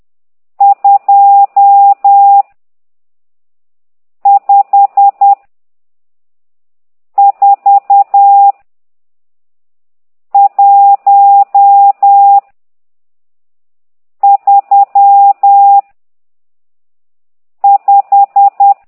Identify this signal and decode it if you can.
Morse code, heard as a keyed tone, about 10 wpm, 800 Hz. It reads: 2 5 4 1 3 5
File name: ct2idw / exerc12